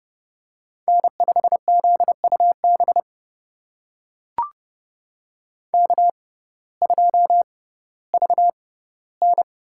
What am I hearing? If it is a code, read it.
Morse code, heard as a keyed tone, about 30 wpm, spaced as for 17 wpm, 700 Hz. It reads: N5ZUB K2VN
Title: USA Call Signs - 6x Character Spacing - 30wpm